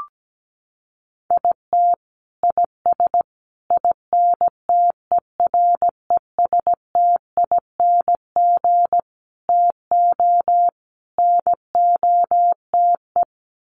Morse code, heard as a keyed tone, about 17 wpm, 700 Hz. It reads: IT IS INTERESTING TO NOTE
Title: Sets of 5 Words Encore2 17wpm